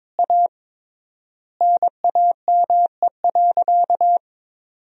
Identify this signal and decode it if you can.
Morse code, heard as a keyed tone, about 22 wpm, 700 Hz. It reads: A NAME.
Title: 3x Word-Spacing - Sentences from Top 500 Words - 22wpm